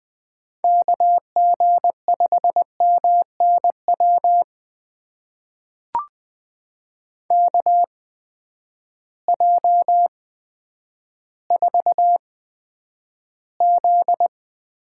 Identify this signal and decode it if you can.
Morse code, heard as a keyed tone, about 20 wpm, spaced as for 10 wpm, 700 Hz. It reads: KG5MNW KJ4Z
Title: USA Call Signs - 6 Characters - 8x Character Spacing - 20wpm